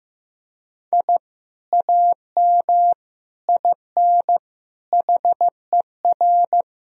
Morse code, heard as a keyed tone, about 15 wpm, 700 Hz. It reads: I AM IN HER
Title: Sentences from Top 1000 Words - Review Part 4 - 15wpm